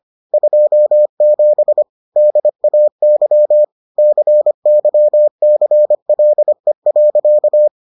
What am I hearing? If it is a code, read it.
Morse code, heard as a keyed tone, about 25 wpm, 600 Hz. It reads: 27 DAY CYCLE.